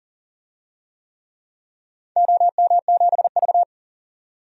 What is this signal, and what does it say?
Morse code, heard as a keyed tone, about 41 wpm, 700 Hz. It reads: OM7V
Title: DXCC-100 Call Signs - 4 Characters - 40wpm